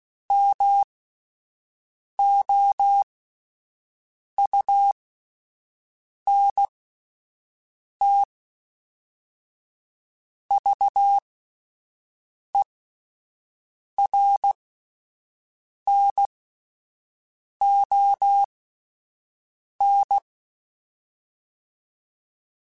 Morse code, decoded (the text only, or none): MOUNTVERNON